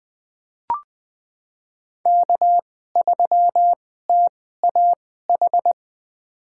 Morse code, heard as a keyed tone, about 20 wpm, spaced as for 14 wpm, 700 Hz. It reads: K3TAH
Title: USA Call Signs - 2x Character Spacing - 20wpm